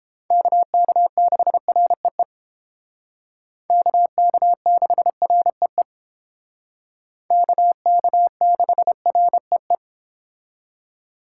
Morse code, heard as a keyed tone, about 30 wpm, 700 Hz. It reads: KK6REE KK6REE KK6REE